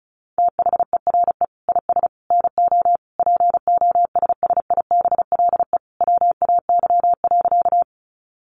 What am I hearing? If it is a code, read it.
Morse code, heard as a keyed tone, about 34 wpm, 700 Hz. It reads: THERE IS NO POSSIBLE WAY.
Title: Sentences from Top 500 Words 35wpm